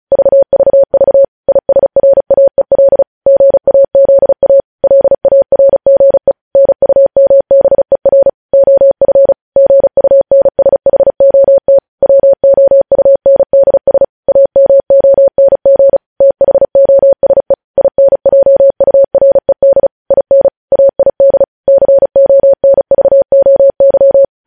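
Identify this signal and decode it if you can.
Morse code, heard as a keyed tone, about 36 wpm, 550 Hz. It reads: VVV ISRAEL GAZA LARGE NUMBER OF GUNSHOT WOUNDS AMONG THOSE INJURED IN AID CONVOY
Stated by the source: Morse code news headlines sent at 35.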